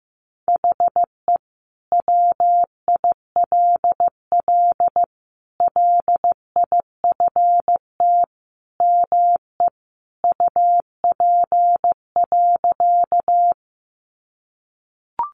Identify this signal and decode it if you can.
Morse code, heard as a keyed tone, about 15 wpm, 700 Hz. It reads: HE WILL LIFT ME UP.
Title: Sentences from Top 1000 Words 15wpm